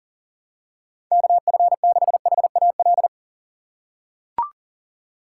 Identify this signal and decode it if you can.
Morse code, heard as a keyed tone, about 37 wpm, 700 Hz. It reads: KF6HAL
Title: USA Call Signs - 6 Characters - 6x Character Spacing - 40wpm